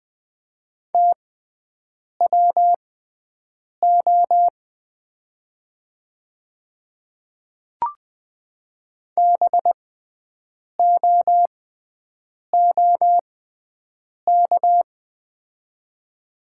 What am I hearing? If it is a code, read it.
Morse code, heard as a keyed tone, about 20 wpm, spaced as for 7 wpm, 690 Hz. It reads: TWO BOOK
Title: 6x Character Spacing - Top 300 Words - 20wpm